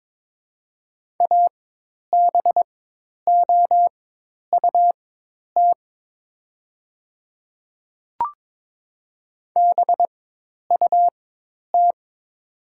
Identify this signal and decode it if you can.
Morse code, heard as a keyed tone, about 22 wpm, spaced as for 10 wpm, 700 Hz. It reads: ABOUT BUT
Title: 4x Character Spacing - Top 200 Words - 22wpm